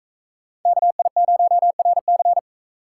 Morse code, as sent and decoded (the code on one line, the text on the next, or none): -.- .. ----- .-. -.-.
KI0RC